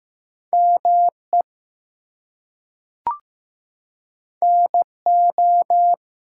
Morse code, heard as a keyed tone, about 15 wpm, 700 Hz. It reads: ME NO
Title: Two Letter Words - 15wpm